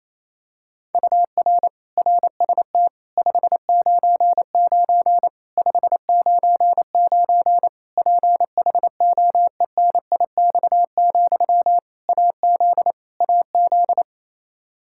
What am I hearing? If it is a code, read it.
Morse code, heard as a keyed tone, about 28 wpm, 700 Hz. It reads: UR RST 599 599 PHOENIX, AZ AZ
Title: K1USN SST QSO Practice 28wpm